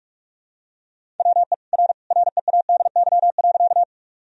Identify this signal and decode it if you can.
Morse code, heard as a keyed tone, about 46 wpm, 700 Hz. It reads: WE R READY.
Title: Sentences from Top 400 Words and Abbreviations 45wpm